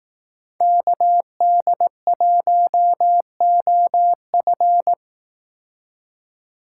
Morse code, heard as a keyed tone, about 18 wpm, 700 Hz. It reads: KD1OF
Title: Speed Racing - USA 2x2 Call Signs 15wpm